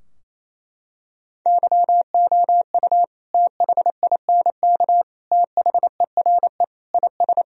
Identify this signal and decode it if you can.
Morse code, heard as a keyed tone, about 28 wpm, 700 Hz. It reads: YOU THINK THERE IS
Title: Sentences from Top 1000 Words - Review Part 4 - 28wpm